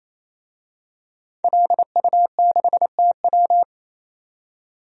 Morse code, heard as a keyed tone, about 28 wpm, 700 Hz. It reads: LU6TW
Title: DXCC-100 Call Signs - 5 Characters - 28wpm